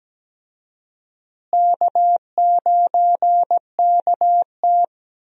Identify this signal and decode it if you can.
Morse code, heard as a keyed tone, about 17 wpm, 700 Hz. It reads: K9KT
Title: USA Call Signs - 4 Characters - Repeated 2x - 17wpm